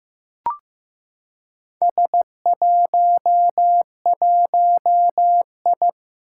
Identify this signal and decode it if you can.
Morse code, heard as a keyed tone, about 15 wpm, 700 Hz. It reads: S11I
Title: Four Challenging Letter-Number Combinations 15wpm